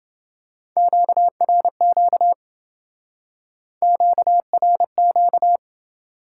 Morse code, heard as a keyed tone, about 27 wpm, 700 Hz. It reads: QRQ QRQ